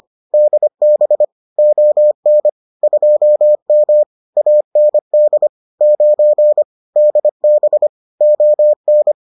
Morse code, heard as a keyed tone, about 25 wpm, 600 Hz. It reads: DB ON 2M AND 9 DB ON